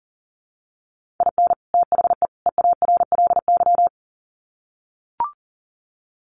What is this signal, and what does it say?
Morse code, heard as a keyed tone, about 40 wpm, 700 Hz. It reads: IN THE EARLY
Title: Sets of 3 Words 40wpm